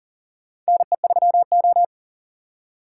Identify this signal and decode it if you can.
Morse code, heard as a keyed tone, about 40 wpm, 700 Hz. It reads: NE3O
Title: USA Call Signs - 4x Character Spacing - 40wpm